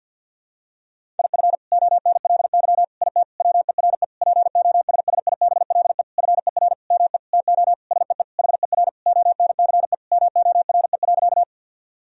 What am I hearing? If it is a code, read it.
Morse code, heard as a keyed tone, about 53 wpm, 700 Hz. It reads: IF ONLY IT WERE POSSIBLE FER ME TO SEE HER ONCE MORE.
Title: Sentences from Top 500 Words and Abbreviations 50wpm